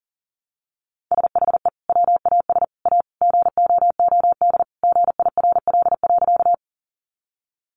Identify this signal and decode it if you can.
Morse code, heard as a keyed tone, about 40 wpm, 700 Hz. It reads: SHE WAS A GOOD GIRL.
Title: Sentences from Top 400 Words 40wpm